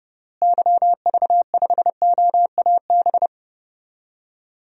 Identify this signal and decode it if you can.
Morse code, heard as a keyed tone, about 32 wpm, 700 Hz. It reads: YV5OAB
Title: DXCC-100 Call Signs - 6 Characters - Repeated 3x - 30wpm